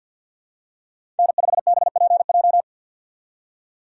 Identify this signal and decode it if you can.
Morse code, heard as a keyed tone, about 52 wpm, 700 Hz. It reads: N5BPJ
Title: USA Call Signs - 5 Characters - 3x Character Spacing - 50wpm